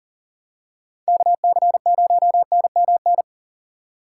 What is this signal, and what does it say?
Morse code, heard as a keyed tone, about 40 wpm, 700 Hz. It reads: KC0NMN
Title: USA Call Signs - 6 Characters - 6x Character Spacing - 40wpm